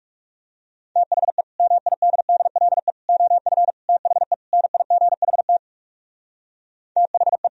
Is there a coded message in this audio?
Yes: Morse code from 1.0 to 7.5 s, heard as a keyed tone, about 38 wpm, 700 Hz. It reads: THE MIDDLE OF THE NIGHT THE